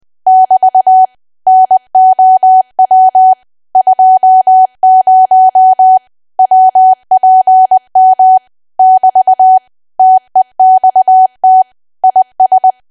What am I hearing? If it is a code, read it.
Morse code, heard as a keyed tone, about 20 wpm, 750 Hz. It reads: <BT> NOW 20 WPM <BT> TEXT IS
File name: cwa / W1AW-Level3 / 20WPM